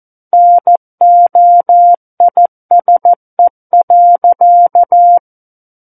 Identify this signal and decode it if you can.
Morse code, heard as a keyed tone, about 14 wpm, 700 Hz. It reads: NOISE.
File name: Book One / 05-Chapter 5 - 14 WPM